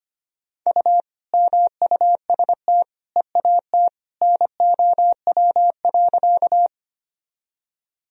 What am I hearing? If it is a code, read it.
Morse code, heard as a keyed tone, about 25 wpm, 700 Hz. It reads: U MUST EAT NOW.
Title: Sentences from Top 300 Words and Abbreviations 25wpm